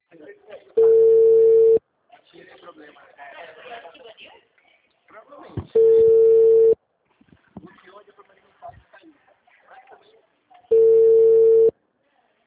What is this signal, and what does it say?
Ringback tone (China, 450 Hz), with speech.
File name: out- -2018- - - .8